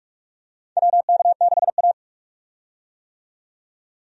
Morse code, heard as a keyed tone, about 44 wpm, 700 Hz. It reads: WK6A
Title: USA Call Signs - 4 Characters - 45wpm